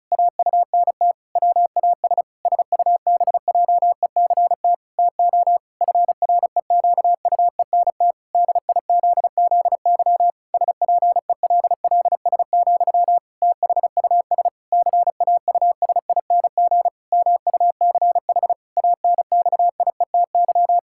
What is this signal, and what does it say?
Morse code, heard as a keyed tone, about 36 wpm, 700 Hz. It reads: AUNT WAS SUBJECT TO FREQUENT DIZZY SPELLS, THUS CAUSING MUCH ANXIETY
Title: Pangram Challenge v2 - 35wpm